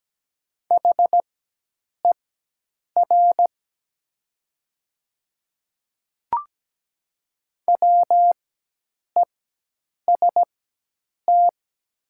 Morse code, heard as a keyed tone, about 17 wpm, spaced as for 8 wpm, 700 Hz. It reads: HER WEST
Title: 4x Character Spacing - Top 500 Words - 17wpm